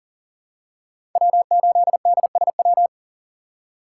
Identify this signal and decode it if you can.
Morse code, heard as a keyed tone, about 40 wpm, 700 Hz. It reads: W8DSW
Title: USA Call Signs - 5 Characters - 3x Character Spacing - 40wpm